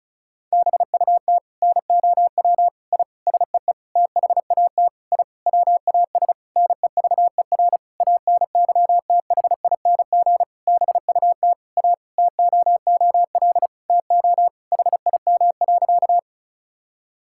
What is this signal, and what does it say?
Morse code, heard as a keyed tone, about 36 wpm, 690 Hz. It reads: BUT NOW I SEE THAT I WAS NEVER ANYTHING BUT A TOOL TO HIM.
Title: Sentences from Top 1000 Words - Review Part 6 - 35wpm